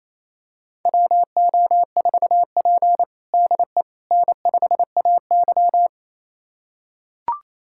Morse code, heard as a keyed tone, about 28 wpm, 700 Hz. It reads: WO4P DE N5AY
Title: USA Callsign DE Callsign - 4 Characters - 28wpm